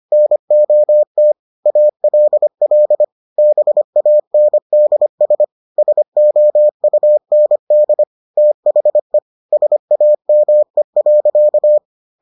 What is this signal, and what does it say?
Morse code, heard as a keyed tone, about 25 wpm, 600 Hz. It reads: NOT ALL BANDS SOUND THE SAME.